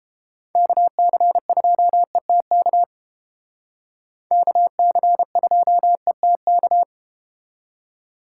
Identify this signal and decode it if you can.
Morse code, heard as a keyed tone, about 30 wpm, 700 Hz. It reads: KC2ETK KC2ETK